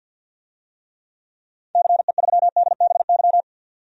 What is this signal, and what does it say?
Morse code, heard as a keyed tone, about 52 wpm, 700 Hz. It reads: CE3DBY
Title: DXCC-100 Call Signs - 6 Characters - 50wpm